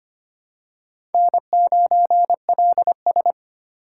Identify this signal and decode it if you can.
Morse code, heard as a keyed tone, about 25 wpm, 700 Hz. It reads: N9LS
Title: USA Call Signs - 4 Characters - 3x Character Spacing - 25wpm